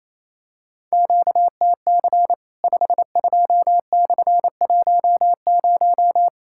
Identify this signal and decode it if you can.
Morse code, heard as a keyed tone, about 28 wpm, 700 Hz. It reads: QTC 52/10
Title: Worked All Europe DX Contest - QTC Exchange - 28wpm